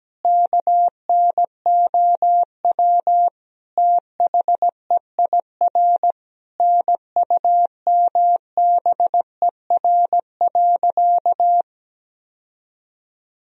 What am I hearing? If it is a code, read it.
Morse code, heard as a keyed tone, about 17 wpm, 700 Hz. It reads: KNOW THEIR NUMBER.
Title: Sentences from Top 1000 Words - Review Part 1 - 17wpm